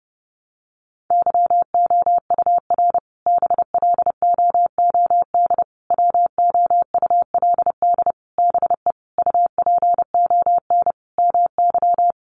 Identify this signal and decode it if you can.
Morse code, heard as a keyed tone, about 30 wpm, 700 Hz. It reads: YOUR BLOOD WOULD BE UPON MY
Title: Sentences from Top 1100 Words 30wpm